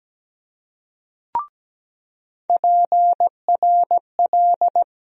Morse code, heard as a keyed tone, about 17 wpm, 700 Hz. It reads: PRL
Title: USA Call Sign Postfixes - 3 Letters 17wpm